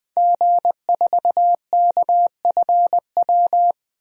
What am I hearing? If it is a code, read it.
Morse code, heard as a keyed tone, about 21 wpm, 700 Hz. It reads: G4KFW
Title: DXCC-100 Call Signs - 5 Characters - Repeated 2x - 20wpm